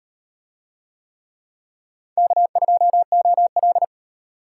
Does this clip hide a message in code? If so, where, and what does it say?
Yes: Morse code from 2.2 to 3.8 s, heard as a keyed tone, about 34 wpm, 700 Hz. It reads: K2OL